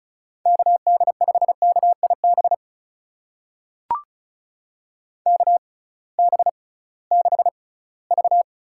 Morse code, heard as a keyed tone, about 36 wpm, spaced as for 20 wpm, 700 Hz. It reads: KD5KIB KB6V
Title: USA Call Signs - 6 Characters - 6x Character Spacing - 35wpm